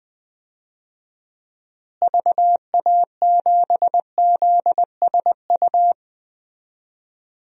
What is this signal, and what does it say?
Morse code, heard as a keyed tone, about 20 wpm, 700 Hz. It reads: VA7ZSU
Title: DXCC-100 Call Signs - 6 Characters - Repeated 2x - 20wpm